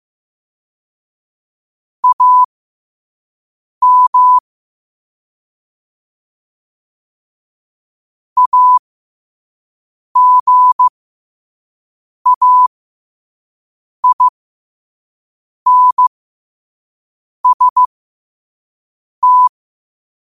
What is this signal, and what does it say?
Morse code, heard as a keyed tone, about 15 wpm, spaced as for 5 wpm, 1000 Hz. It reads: AM AGAINST
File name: qotdfiles / 5wpm